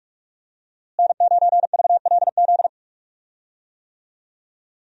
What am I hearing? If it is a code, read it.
Morse code, heard as a keyed tone, about 46 wpm, 700 Hz. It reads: N9VLZ
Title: USA Call Signs - 5 Characters - Repeated 2x - 45wpm